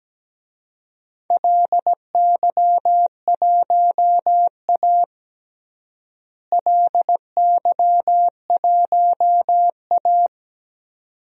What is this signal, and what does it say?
Morse code, heard as a keyed tone, about 17 wpm, 700 Hz. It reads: LY1A LY1A
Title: DXCC-100 Call Signs - 4 Characters - Repeated 2x - 17wpm